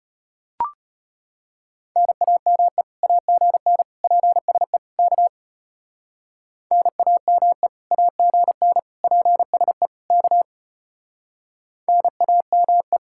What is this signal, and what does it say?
Morse code, heard as a keyed tone, about 30 wpm, 700 Hz. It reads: NAME AGN PSE K NAME AGN PSE K NAME
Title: Speed-Racing Common QSO Phrases - 25wpm